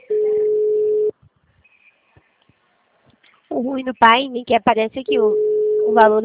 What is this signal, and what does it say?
Ringback tone (Europe, 425 Hz), with speech.
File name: out- -2016- - - .1726